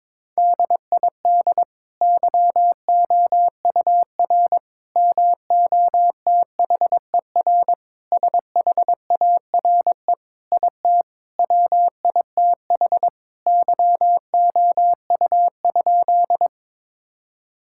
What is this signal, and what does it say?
Morse code, heard as a keyed tone, about 22 wpm, 700 Hz. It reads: DID YOUR MOTHER SHARE IT WITH YOU?
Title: Sentences from Top 1000 Words - Review Part 6 - 22wpm